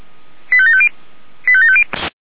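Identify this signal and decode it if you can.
Five-tone selective call (ZVEI), heard as quick runs of five tones, tones of about 70 ms. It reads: 85750 85750
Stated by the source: ZVEI five-tone call received by SDR.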